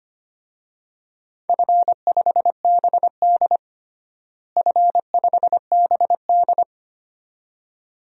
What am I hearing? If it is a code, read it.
Morse code, heard as a keyed tone, about 25 wpm, 700 Hz. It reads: F5BD F5BD
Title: DXCC-100 Call Signs - 4 Characters - Repeated 2x - 25wpm